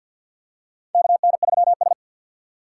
Morse code, heard as a keyed tone, about 47 wpm, 700 Hz. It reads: KN3S
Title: USA Call Signs - 4 Characters - 2x Character Spacing - 50wpm